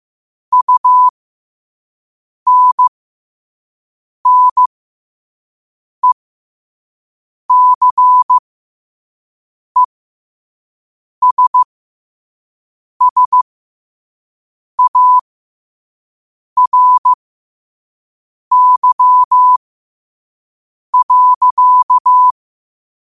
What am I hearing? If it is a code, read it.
Morse code, heard as a keyed tone, about 15 wpm, spaced as for 5 wpm, 1000 Hz. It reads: UNNECESSARY.
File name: qotdfiles / 5wpm